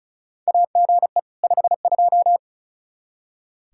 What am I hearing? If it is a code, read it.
Morse code, heard as a keyed tone, about 36 wpm, 700 Hz. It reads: AGE 52
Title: Practice QSO - Exchange 3 - More Info - 35wpm